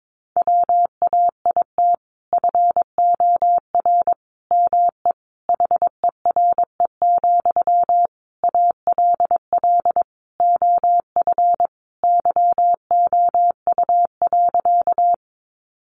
Morse code, heard as a keyed tone, about 22 wpm, 700 Hz. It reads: WAIT FOR ME HERE, ALL OF YOU.
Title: Sentences from Top 500 Words 22wpm